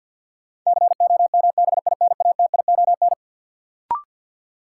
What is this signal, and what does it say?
Morse code, heard as a keyed tone, about 53 wpm, 700 Hz. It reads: COMBINATION
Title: Sentences from Top 1000 Words - Review Part 6 - New Word Drill - 50wpm